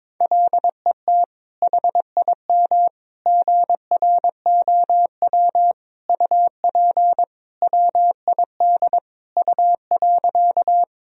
Morse code, heard as a keyed tone, about 22 wpm, 700 Hz. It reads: LET HIM GROW UP WID U.
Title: Sentences from Top 300 Words and Abbreviations 22wpm